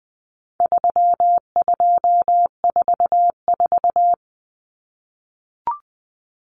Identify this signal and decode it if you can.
Morse code, heard as a keyed tone, about 20 wpm, 700 Hz. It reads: 3244
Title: Common Cut Numbers - 4-Digit 20wpm